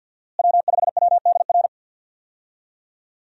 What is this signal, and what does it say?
Morse code, heard as a keyed tone, about 52 wpm, 700 Hz. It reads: W5WDR
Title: USA Call Signs - 5 Characters - Repeated 2x - 50wpm